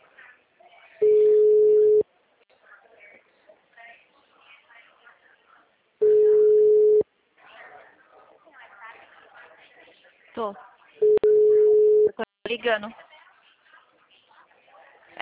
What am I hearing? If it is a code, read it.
Ringback tone (Europe, 425 Hz), with speech.